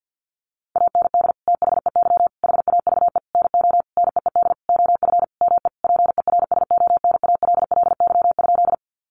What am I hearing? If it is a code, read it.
Morse code, heard as a keyed tone, about 50 wpm, 700 Hz. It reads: AND THEY HAVE NO NEED OF ME PERSONALLY?
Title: Sentences from Top 1400 Words 50wpm